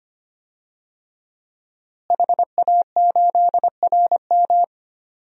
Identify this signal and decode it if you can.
Morse code, heard as a keyed tone, about 25 wpm, 700 Hz. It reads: HA8RM
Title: DXCC-100 Call Signs - 5 Characters - Repeated 2x - 25wpm